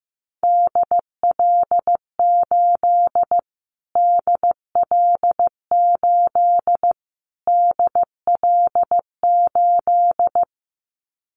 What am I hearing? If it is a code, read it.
Morse code, heard as a keyed tone, about 15 wpm, 700 Hz. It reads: DL8 DL8 DL8
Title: Callsign Prefixes for Top 25 DXCC - Repeated 3x - 15wpm